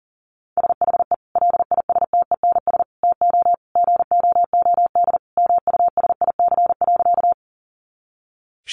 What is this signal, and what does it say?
Morse code, heard as a keyed tone, about 40 wpm, 700 Hz. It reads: SHE LISTENS TO GOOD MUSIC.
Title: Easy Phrases 40wpm